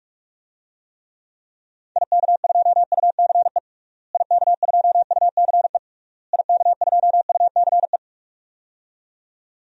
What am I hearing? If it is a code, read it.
Morse code, heard as a keyed tone, about 46 wpm, 700 Hz. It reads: IK2UCE IK2UCE IK2UCE
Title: DXCC-100 Call Signs - 6 Characters - Repeated 3x - 45wpm